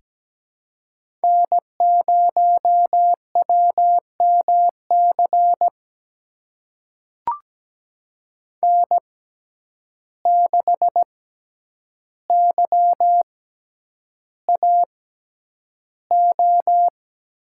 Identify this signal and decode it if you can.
Morse code, heard as a keyed tone, about 17 wpm, spaced as for 9 wpm, 700 Hz. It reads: N0WMC N6YAO
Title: USA Call Signs - 5 Characters - 6x Character Spacing - 17wpm